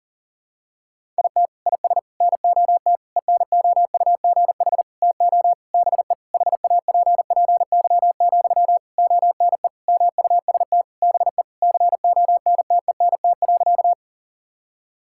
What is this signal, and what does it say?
Morse code, heard as a keyed tone, about 40 wpm, 700 Hz. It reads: IT IS NOT ENOUGH TO BE HAPPY, ONE MUST BE CONTENT.
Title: Sentences from Top 1000 Words - Review Part 6 - 40wpm